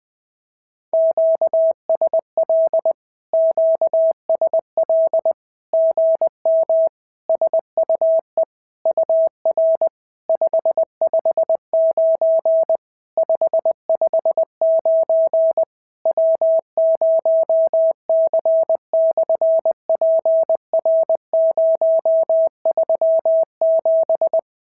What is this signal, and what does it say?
Morse code, heard as a keyed tone, about 20 wpm, 650 Hz. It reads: QSL QSL GM SUE UR 559 559 W0C/PR037
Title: SOTA QSO Practice 20wpm